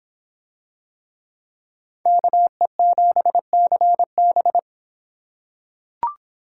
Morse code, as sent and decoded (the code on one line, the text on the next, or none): -.- . --... -.-. -...
KE7CB